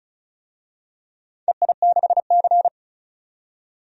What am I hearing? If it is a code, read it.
Morse code, heard as a keyed tone, about 34 wpm, 700 Hz. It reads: EI6C
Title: DXCC-100 Call Signs - 4 Characters - 35wpm